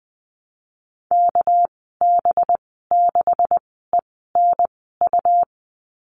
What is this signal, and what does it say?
Morse code, heard as a keyed tone, about 20 wpm, spaced as for 14 wpm, 700 Hz. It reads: KB6ENU
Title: 238 Lesson 38g - Letter q - Call Signs 20wpm